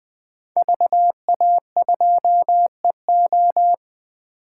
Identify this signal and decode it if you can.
Morse code, heard as a keyed tone, about 20 wpm, 700 Hz. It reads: VA2EO
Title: 2023 SOTA Chaser Call Signs - 20wpm